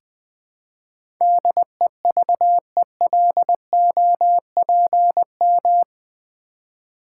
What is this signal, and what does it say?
Morse code, heard as a keyed tone, about 20 wpm, 700 Hz. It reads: DEVELOPM
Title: Build A Word - 4 Letters At A Time - 11-Letter Words - 20wpm